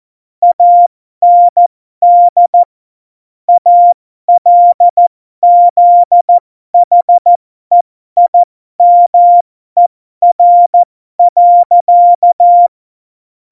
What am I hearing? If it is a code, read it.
Morse code, heard as a keyed tone, about 14 wpm, spaced as for 12 wpm, 700 Hz. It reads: AND ALZHEIMER.
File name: Morsekurs / Witze